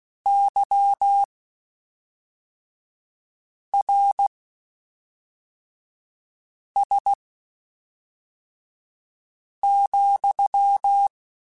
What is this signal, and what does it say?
Morse code, heard as a keyed tone, about 16 wpm, spaced as for 3 wpm, 800 Hz. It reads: YRS,
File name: (spaced) / QSOs Spaced-mp3s / WD0BGP 1 @5 sp